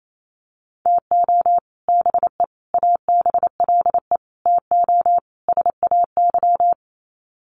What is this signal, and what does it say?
Morse code, heard as a keyed tone, about 28 wpm, 700 Hz. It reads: TO BE ABLE TO SAY
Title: Sets of 5 Words Encore2 28wpm